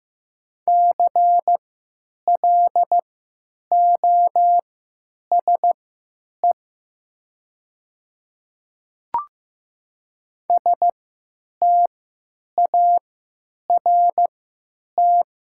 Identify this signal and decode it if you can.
Morse code, heard as a keyed tone, about 15 wpm, spaced as for 9 wpm, 700 Hz. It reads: CLOSE START
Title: 3x Character Spacing - Top 300 Words - 15wpm